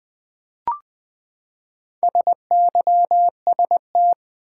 Morse code, heard as a keyed tone, about 20 wpm, 700 Hz. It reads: SYST
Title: Build A Word - 4 Letters At A Time - 10-Letter Words - 20wpm